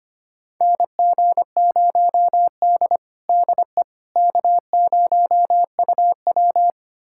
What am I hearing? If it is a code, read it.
Morse code, heard as a keyed tone, about 25 wpm, 700 Hz. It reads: NG0D DE K0UW
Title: USA Callsign DE Callsign - 4 Characters - 25wpm